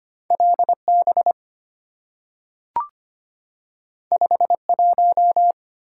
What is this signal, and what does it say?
Morse code, heard as a keyed tone, about 25 wpm, 700 Hz. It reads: LB 51
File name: Two Letter-Number Combinations - 25wpm